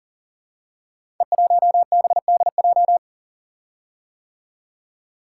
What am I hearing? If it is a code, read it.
Morse code, heard as a keyed tone, about 39 wpm, 700 Hz. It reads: E1BDJ
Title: Code Groups - 5 Letter:Numbers - 40wpm